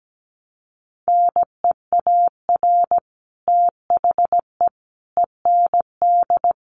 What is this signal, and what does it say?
Morse code, heard as a keyed tone, about 17 wpm, 700 Hz. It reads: NEAR THE END
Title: Sets of 5 Words Encore 17wpm